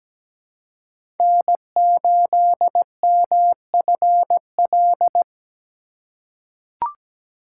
Morse code, heard as a keyed tone, about 17 wpm, 700 Hz. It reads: N8MFL
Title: USA Call Signs - 5 Characters - 2x Character Spacing - 17wpm